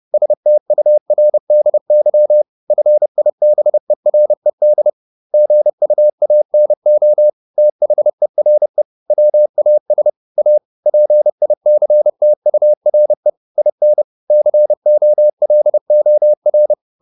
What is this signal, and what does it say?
Morse code, heard as a keyed tone, about 30 wpm, 600 Hz. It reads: STURDY FIBERED GUANO THERE WAS A PICTURE IN COLOR
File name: the old man and the sea / the old man and the sea 1 30wpm